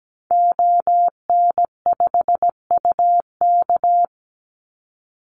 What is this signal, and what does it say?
Morse code, heard as a keyed tone, about 17 wpm, 700 Hz. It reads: ON5UK
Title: Call Signs - 17wpm